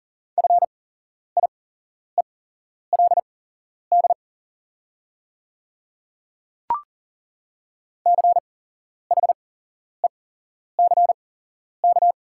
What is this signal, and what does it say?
Morse code, heard as a keyed tone, about 40 wpm, spaced as for 11 wpm, 690 Hz. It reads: FIELD CHECK
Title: 8x Character Spacing - Top 500 Words - 40wpm